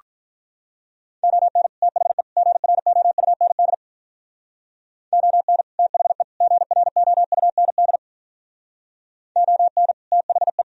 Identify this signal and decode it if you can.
Morse code, heard as a keyed tone, about 43 wpm, 700 Hz. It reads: ON THE GROUND ON THE GROUND ON THE
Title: Speed-Racing - Sets of 3 Words 35wpm